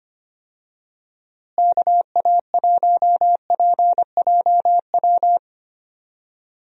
Morse code, heard as a keyed tone, about 25 wpm, 700 Hz. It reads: KA1PJW
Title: USA Call Signs - 6 Characters - Repeated 3x - 25wpm